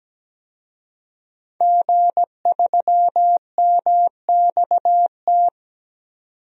Morse code, heard as a keyed tone, about 17 wpm, 700 Hz. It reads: G3MXT
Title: DXCC-100 Call Signs - 5 Characters - 17wpm